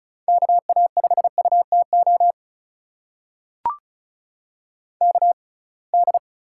Morse code, heard as a keyed tone, about 36 wpm, spaced as for 27 wpm, 700 Hz. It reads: KA5UTO KD
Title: USA Call Signs - 6 Characters - 6x Character Spacing - 35wpm